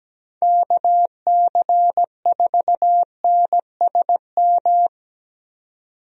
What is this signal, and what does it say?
Morse code, heard as a keyed tone, about 17 wpm, 700 Hz. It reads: KC4NSM
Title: USA Call Signs - 6 Characters - 17wpm